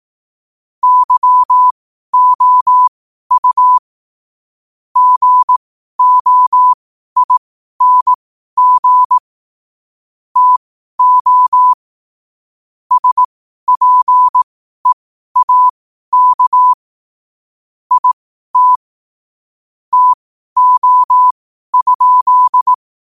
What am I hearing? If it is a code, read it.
Morse code, heard as a keyed tone, about 18 wpm, spaced as for 12 wpm, 1000 Hz. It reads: YOU GOING TO SPEAK IT TO?
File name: qotdfiles / 12wpm